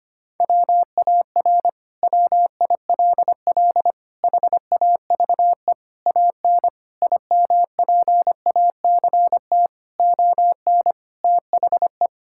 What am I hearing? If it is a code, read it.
Morse code, heard as a keyed tone, about 25 wpm, 700 Hz. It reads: WAR WILL HAVE AN IMPACT ON THE
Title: Sentences from Top 1000 Words - Review Part 6 - 25wpm